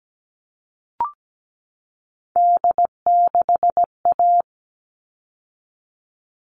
Morse code, heard as a keyed tone, about 17 wpm, 700 Hz. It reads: D6A
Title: All Cut Numbers - 3-Digit 17wpm